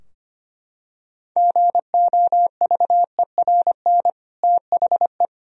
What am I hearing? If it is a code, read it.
Morse code, heard as a keyed tone, about 25 wpm, 700 Hz. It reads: GOVERN THE
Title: Sentences from Top 1000 Words - Review Part 2 - 25wpm